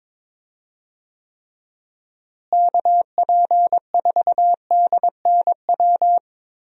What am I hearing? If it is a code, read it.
Morse code, heard as a keyed tone, about 21 wpm, 700 Hz. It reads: KP4DNW
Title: USA Call Signs - 6 Characters - Repeated 3x - 22wpm